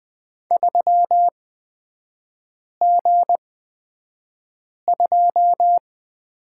Morse code, heard as a keyed tone, about 20 wpm, spaced as for 5 wpm, 700 Hz. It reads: 3G2
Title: 236 Lesson 38e - Letter q - Three Characters 20wpm